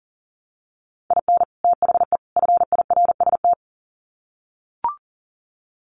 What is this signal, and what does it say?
Morse code, heard as a keyed tone, about 40 wpm, 700 Hz. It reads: IN THE FIRST
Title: Sets of 3 Words 40wpm